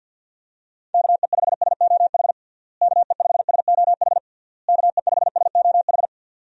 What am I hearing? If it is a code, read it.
Morse code, heard as a keyed tone, about 50 wpm, 700 Hz. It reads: KE5SOH KE5SOH KE5SOH
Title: USA Call Signs - 6 Characters - Repeated 3x - 50wpm